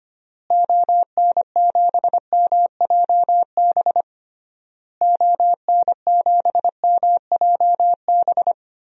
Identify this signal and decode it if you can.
Morse code, heard as a keyed tone, about 25 wpm, 690 Hz. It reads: ON7MJB ON7MJB
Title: DXCC-100 Call Signs - 6 Characters - Repeated 3x - 25wpm